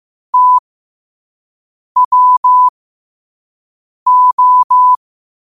Morse code, heard as a keyed tone, about 15 wpm, spaced as for 5 wpm, 1000 Hz. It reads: TWO
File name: qotdfiles / 5wpm